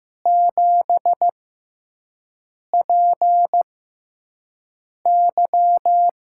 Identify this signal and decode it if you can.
Morse code, heard as a keyed tone, about 15 wpm, spaced as for 5 wpm, 700 Hz. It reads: 7PY
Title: USA Call Signs - 4 Characters - 6x Character Spacing - 15wpm